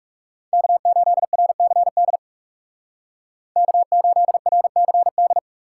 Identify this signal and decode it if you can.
Morse code, heard as a keyed tone, about 35 wpm, 700 Hz. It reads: K8RCD K8RCD
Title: Speed Racing - USA 1x3 Call Signs 30wpm